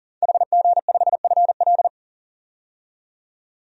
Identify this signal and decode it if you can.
Morse code, heard as a keyed tone, about 40 wpm, 700 Hz. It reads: HG5FL
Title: DXCC-100 Call Signs - 5 Characters - 40wpm